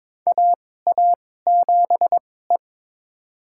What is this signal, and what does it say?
Morse code, heard as a keyed tone, about 22 wpm, spaced as for 16 wpm, 700 Hz. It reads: AA7E
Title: USA Call Signs - 2x Character Spacing - 22wpm